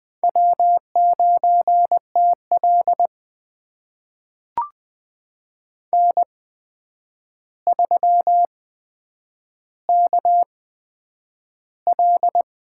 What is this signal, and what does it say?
Morse code, heard as a keyed tone, about 21 wpm, spaced as for 9 wpm, 700 Hz. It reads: W9TL N3KL
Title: USA Call Signs - 4 Characters - 8x Character Spacing - 20wpm